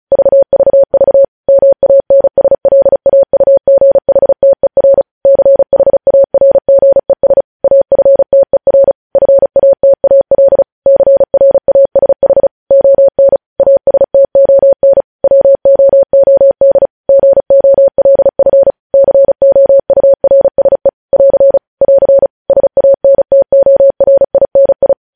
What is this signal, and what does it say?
Morse code, heard as a keyed tone, about 36 wpm, 550 Hz. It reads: VVV MANSLAUGHTER CHARGES AFTER FATAL CRASH ON ASTON WOOD GOLF COURSE <AR> <AR> SANTORINI
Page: news headlines in morse code at 35 wpm